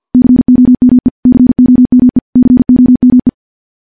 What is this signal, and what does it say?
Morse code, heard as a keyed tone, about 51 wpm, 250 Hz. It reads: COME COME COME
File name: morse / morse-training-50WPM